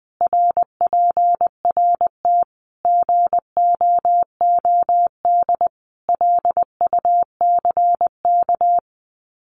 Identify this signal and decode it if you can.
Morse code, heard as a keyed tone, about 20 wpm, 700 Hz. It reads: RPRT GOOD LUCK
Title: Common QSO Phrases 20wpm